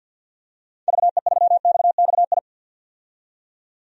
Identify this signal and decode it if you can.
Morse code, heard as a keyed tone, about 47 wpm, 700 Hz. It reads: VE3XXI
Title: DXCC-100 Call Signs - 6 Characters - 50wpm